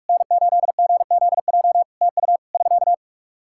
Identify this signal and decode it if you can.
Morse code, heard as a keyed tone, about 45 wpm, 700 Hz. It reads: N8GZJ TU <SK>
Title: Practice QSO - Exchange 4 - Ending - 45wpm